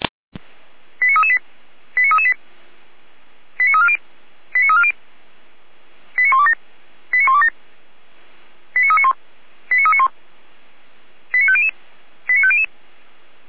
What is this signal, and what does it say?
Five-tone selective call (ZVEI), heard as quick runs of five tones, tones of about 70 ms. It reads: 89338 89338 89340 89340 89127 89127 89391 89391 89500 89500